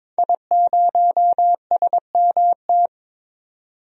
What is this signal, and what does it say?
Morse code, heard as a keyed tone, about 22 wpm, 700 Hz. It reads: I0SMT
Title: DXCC-100 Call Signs - 5 Characters - 22wpm